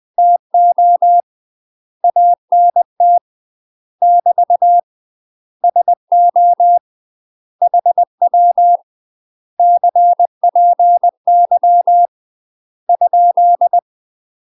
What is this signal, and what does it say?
Morse code, heard as a keyed tone, about 20 wpm, 700 Hz. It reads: TO ANT <BT> SO HW CPY ?
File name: UK MASTER QSOs TRAINING / 20WPM / UK MASTER QSOs 0028